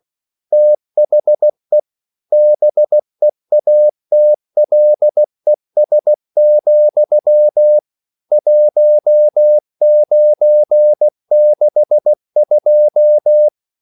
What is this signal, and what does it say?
Morse code, heard as a keyed tone, about 16 wpm, 600 Hz. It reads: THE BEATLES, 1962
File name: CW / Practice / EnglishQuotations0416